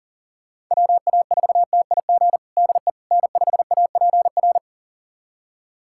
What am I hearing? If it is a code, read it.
Morse code, heard as a keyed tone, about 37 wpm, 700 Hz. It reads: WA4TIG DE N5APR
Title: USA Callsign DE Callsign 40wpm